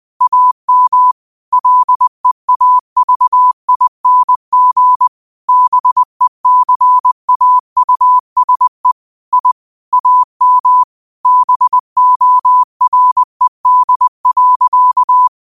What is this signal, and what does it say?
Morse code, heard as a keyed tone, about 20 wpm, 1000 Hz. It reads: AM LEAVING BECAUSE I AM BORED.